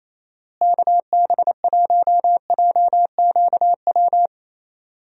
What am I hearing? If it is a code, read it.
Morse code, heard as a keyed tone, about 27 wpm, 700 Hz. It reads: KB1JQW
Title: USA Call Signs - 6 Characters - Repeated 3x - 28wpm